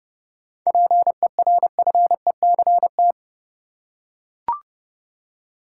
Morse code, heard as a keyed tone, about 30 wpm, 700 Hz. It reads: PERFECT
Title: Seven Letter Words - 30wpm